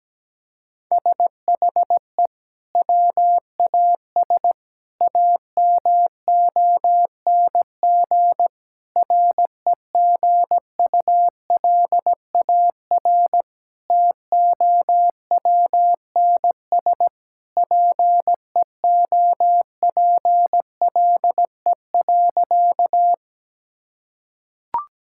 Morse code, heard as a keyed tone, about 17 wpm, 700 Hz. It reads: SHE WAS AMONG REGULAR TOWNS PEOPLE.
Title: Sentences from Top 1000 Words - Review Part 6 - 17wpm